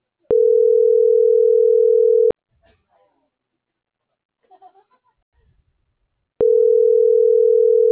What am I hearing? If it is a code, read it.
Ringback tone (North America), 440 + 480 Hz.
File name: CENTRAL MABE / out antonio.garcia